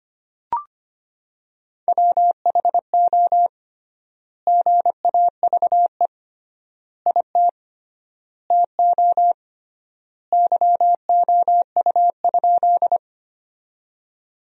Morse code, heard as a keyed tone, about 25 wpm, 700 Hz. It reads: WHO GAVE IT TO YOU?
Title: 3x Word-Spacing - Sentences from Top 500 Words - 25wpm